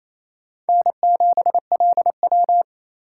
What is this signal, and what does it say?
Morse code, heard as a keyed tone, about 35 wpm, 700 Hz. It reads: N7LW
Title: Speed Racing - USA 1x2 Call Signs 28wpm